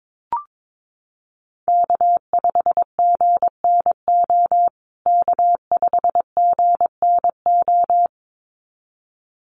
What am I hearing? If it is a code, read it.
Morse code, heard as a keyed tone, about 22 wpm, 700 Hz. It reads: K5GNO K5GNO
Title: USA Call Signs Repeated 2x 22wpm